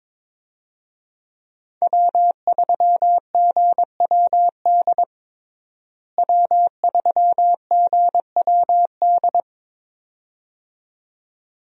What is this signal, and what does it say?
Morse code, heard as a keyed tone, about 22 wpm, 700 Hz. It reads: W3GWD W3GWD
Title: USA Call Signs - 5 Characters - Repeated 2x - 22wpm